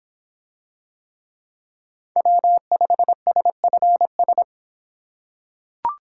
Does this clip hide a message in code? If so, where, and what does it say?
Yes: Morse code from 2.2 to 4.4 s, heard as a keyed tone, about 24 wpm, 700 Hz. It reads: W5SFS